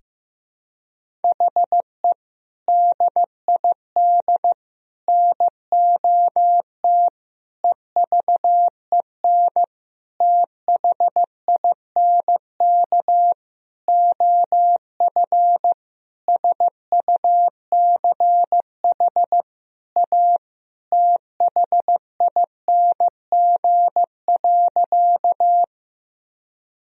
Morse code, heard as a keyed tone, about 16 wpm, 700 Hz. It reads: HE DID NOT EVEN THINK OF SUCH A THING.
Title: Sentences from Top 1000 Words - Review Part 1 - 15wpm